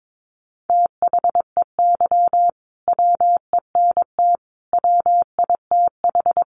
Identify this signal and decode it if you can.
Morse code, heard as a keyed tone, about 22 wpm, 700 Hz. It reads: THEY WENT WITH
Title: Sentences from Top 800 Words 22wpm